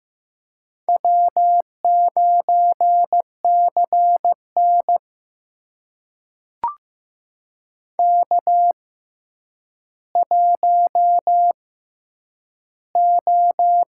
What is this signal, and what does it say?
Morse code, heard as a keyed tone, about 15 wpm, spaced as for 9 wpm, 700 Hz. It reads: W9CN K1O
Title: USA Call Signs - 6x Character Spacing - 15wpm